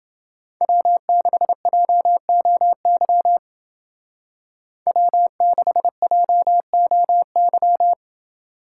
Morse code, heard as a keyed tone, about 28 wpm, 700 Hz. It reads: W6JOY W6JOY